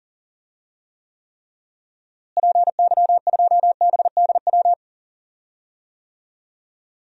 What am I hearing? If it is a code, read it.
Morse code, heard as a keyed tone, about 40 wpm, 700 Hz. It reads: PY2BDW
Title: DXCC-100 Call Signs - 6 Characters - Repeated 3x - 40wpm